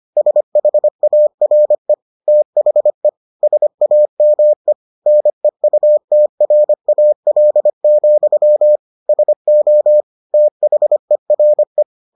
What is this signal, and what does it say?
Morse code, heard as a keyed tone, about 25 wpm, 600 Hz. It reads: SHARE THE SAME NEUTRAL, SO THERE